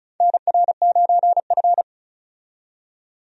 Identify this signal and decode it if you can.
Morse code, heard as a keyed tone, about 36 wpm, 700 Hz. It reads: NR9F
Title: USA Call Signs - 4 Characters - 35wpm